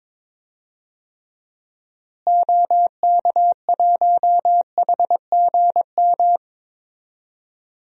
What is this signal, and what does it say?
Morse code, heard as a keyed tone, about 22 wpm, 700 Hz. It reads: OK1HGM
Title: DXCC-100 Call Signs - 6 Characters - Repeated 2x - 22wpm